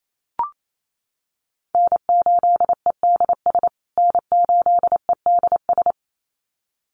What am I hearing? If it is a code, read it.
Morse code, heard as a keyed tone, about 28 wpm, 700 Hz. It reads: N8EDS N8EDS
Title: USA Call Signs Repeated 2x 28wpm